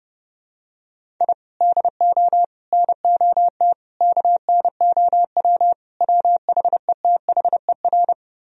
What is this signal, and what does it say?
Morse code, heard as a keyed tone, about 30 wpm, 700 Hz. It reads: I DO NOT KNOW WHETHER